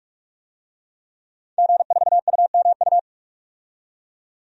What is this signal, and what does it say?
Morse code, heard as a keyed tone, about 42 wpm, 700 Hz. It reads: G4UMU
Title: DXCC-100 Call Signs - 5 Characters - 45wpm